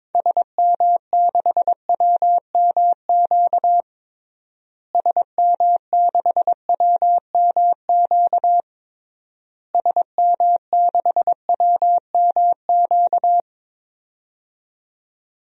Morse code, heard as a keyed tone, about 21 wpm, 700 Hz. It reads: SM6WMQ SM6WMQ SM6WMQ
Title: DXCC-100 Call Signs - 6 Characters - Repeated 3x - 22wpm